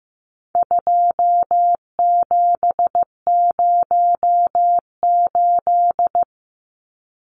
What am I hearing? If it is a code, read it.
Morse code, heard as a keyed tone, about 15 wpm, 700 Hz. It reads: 2708
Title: Numbers - 4-Digit 15wpm